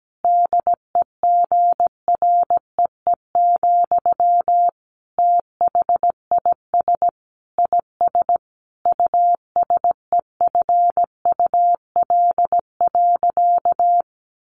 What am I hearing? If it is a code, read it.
Morse code, heard as a keyed tone, about 17 wpm, 700 Hz. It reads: DEGREE, THIS IS USEFUL.
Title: Sentences from Top 1500 Words 17wpm